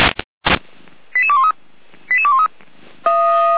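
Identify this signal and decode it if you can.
Five-tone selective call (ZVEI), heard as quick runs of five tones, tones of about 70 ms. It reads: 88213 88213
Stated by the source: ZVEI five-tone selective call received via SDR.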